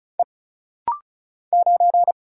Morse code, heard as a keyed tone, about 36 wpm, 700 Hz. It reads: E 9
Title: Single Letter-Number Alt - Rapid-Fire 35wpm